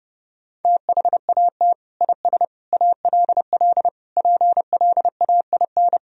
Morse code, heard as a keyed tone, about 32 wpm, 700 Hz. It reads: THAT IS ALL PLAIN
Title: Sentences from Top 1000 Words - Review Part 2 - 30wpm